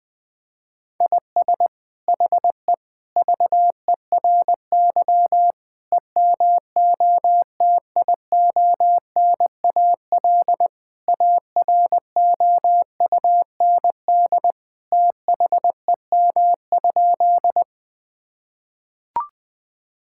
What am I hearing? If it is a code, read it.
Morse code, heard as a keyed tone, about 20 wpm, 700 Hz. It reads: IS HE VERY EMOTIONAL AROUND THEM?
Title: Sentences from Top 1000 Words - Review Part 5 - 20wpm